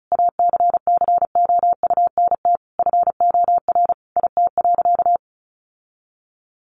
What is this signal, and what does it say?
Morse code, heard as a keyed tone, about 34 wpm, 700 Hz. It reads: ACCOUNT FOR IT.
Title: Sentences from Top 900 Words 35wpm